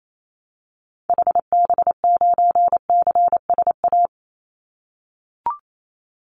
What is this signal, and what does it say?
Morse code, heard as a keyed tone, about 28 wpm, 700 Hz. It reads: HB9CSA
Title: Call Signs - 28wpm